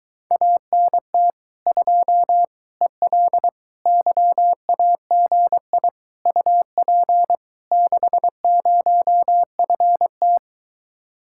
Morse code, heard as a keyed tone, about 23 wpm, 700 Hz. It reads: ANT 2 EL YAGI UP 60FT